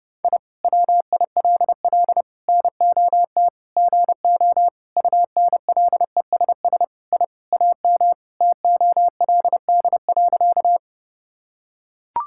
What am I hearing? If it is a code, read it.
Morse code, heard as a keyed tone, about 32 wpm, 700 Hz. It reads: I WILL NOT GO UNLESS I AM TOLD.
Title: Sentences from Top 1000 Words - Review Part 6 - 30wpm